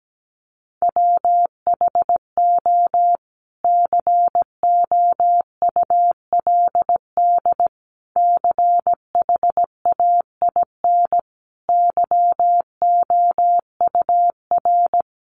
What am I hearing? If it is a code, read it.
Morse code, heard as a keyed tone, about 17 wpm, 700 Hz. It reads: WHO COULD CHAIN YOUR
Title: Sentences from Top 1500 Words 17wpm